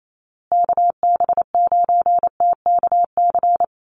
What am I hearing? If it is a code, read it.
Morse code, heard as a keyed tone, about 28 wpm, 700 Hz. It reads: KB9TKC
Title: USA Call Signs Repeated 2x 28wpm